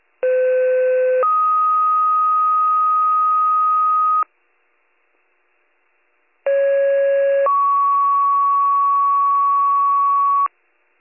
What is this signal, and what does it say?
Two-tone page, heard as a one-second tone, then a three-second tone, about 510 then 1250 Hz, about 550 then 1120 Hz. It is heard from 0.2 to 10.5 s.